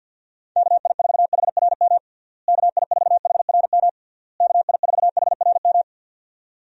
Morse code, heard as a keyed tone, about 49 wpm, 700 Hz. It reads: KI4HRM KI4HRM KI4HRM
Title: USA Call Signs - 6 Characters - Repeated 3x - 50wpm